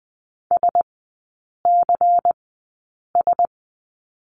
Morse code, heard as a keyed tone, about 20 wpm, spaced as for 8 wpm, 700 Hz. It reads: SCS
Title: 100 Lesson 17e - Letter u - Three Characters 20wpm